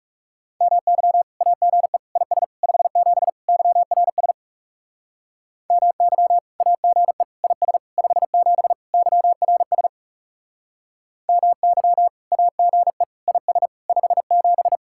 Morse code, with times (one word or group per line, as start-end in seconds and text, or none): 0.6-1.2 MY
1.4-2.0 AGE
2.2-2.4 IS
2.6-3.3 57
3.5-4.3 YRS
5.7-6.4 MY
6.6-7.2 AGE
7.4-7.8 IS
8.0-8.7 57
8.9-9.9 YRS
11.3-12.1 MY
12.3-13.0 AGE
13.3-13.7 IS
13.9-14.8 57